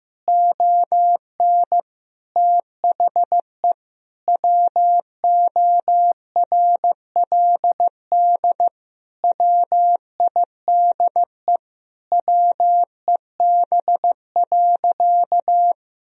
Morse code, heard as a keyed tone, about 15 wpm, 700 Hz. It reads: ON THE WORLD WIDE WEB.
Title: Sentences from Top 1000 Words - Review Part 4 - 15wpm